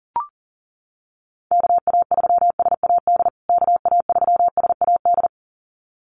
Morse code, heard as a keyed tone, about 40 wpm, 700 Hz. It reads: KA3SAD KA3SAD
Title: USA Call Signs Repeated 2x 40wpm